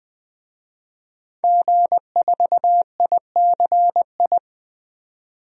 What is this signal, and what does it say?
Morse code, heard as a keyed tone, about 20 wpm, 700 Hz. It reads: G4ICI
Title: DXCC-100 Call Signs - 5 Characters - 20wpm